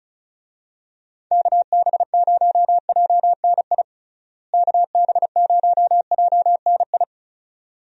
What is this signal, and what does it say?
Morse code, heard as a keyed tone, about 36 wpm, 700 Hz. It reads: KB0JNI KB0JNI
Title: USA Call Signs - 6 Characters - Repeated 2x - 35wpm